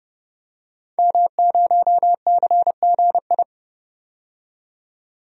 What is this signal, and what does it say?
Morse code, heard as a keyed tone, about 30 wpm, 700 Hz. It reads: M0CGI
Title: DXCC-100 Call Signs - 5 Characters - 30wpm